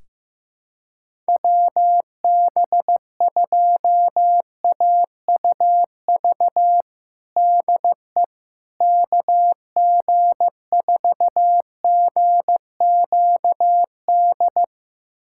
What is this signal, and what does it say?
Morse code, heard as a keyed tone, about 14 wpm, 700 Hz. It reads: WB2AUV DE KG4GQD
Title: USA Callsign DE Callsign - 6 Characters - 15wpm